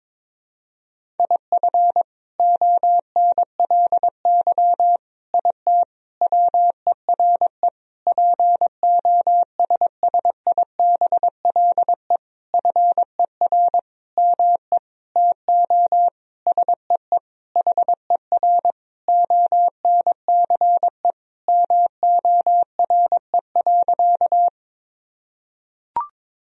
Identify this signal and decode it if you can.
Morse code, heard as a keyed tone, about 22 wpm, 700 Hz. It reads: IF ONLY IT WERE POSSIBLE FER ME TO SEE HER ONCE MORE.
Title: Sentences from Top 500 Words and Abbreviations 22wpm